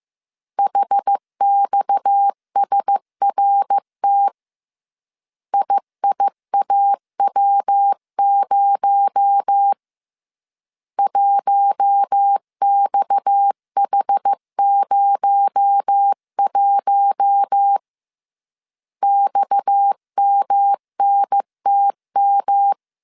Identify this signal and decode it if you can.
Morse code, heard as a keyed tone, about 15 wpm, 800 Hz. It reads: HXSRT IIAW0 1XH01 XMNTM